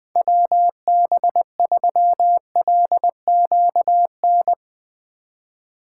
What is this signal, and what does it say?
Morse code, heard as a keyed tone, about 21 wpm, 700 Hz. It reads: WB3LQN
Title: Speed Racing - USA 2x3 Call Signs 17wpm